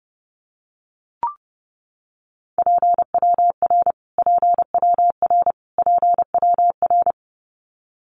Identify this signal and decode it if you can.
Morse code, heard as a keyed tone, about 30 wpm, 700 Hz. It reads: PWR PWR PWR
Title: Abbreviations Repeated 3x - 30wpm